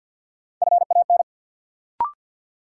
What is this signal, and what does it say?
Morse code, heard as a keyed tone, about 52 wpm, 700 Hz. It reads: FAN
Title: Sentences from Top 1300 Words - New Word Drill Alt - 50wpm